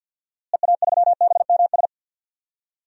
Morse code, heard as a keyed tone, about 52 wpm, 700 Hz. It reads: EA3BMS
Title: DXCC-100 Call Signs - 6 Characters - Repeated 2x - 50wpm